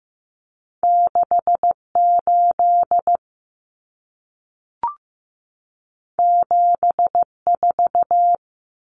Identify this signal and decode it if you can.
Morse code, heard as a keyed tone, about 15 wpm, 700 Hz. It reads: 68 74
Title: Numbers - 2-Digit 15wpm